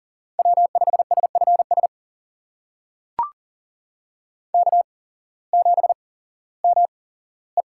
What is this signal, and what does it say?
Morse code, heard as a keyed tone, about 40 wpm, spaced as for 19 wpm, 700 Hz. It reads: W5SFS K7ME
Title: USA Call Signs - 8x Character Spacing - 40wpm